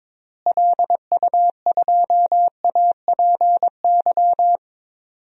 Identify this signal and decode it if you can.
Morse code, heard as a keyed tone, about 22 wpm, 700 Hz. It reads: LU2APY
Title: DXCC-100 Call Signs - 6 Characters - Repeated 2x - 22wpm